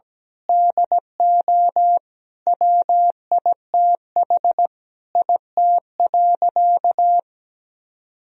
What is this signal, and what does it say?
Morse code, heard as a keyed tone, about 17 wpm, 700 Hz. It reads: DO WITH IT.
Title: Sentences from Top 1000 Words - Review Part 2 - 17wpm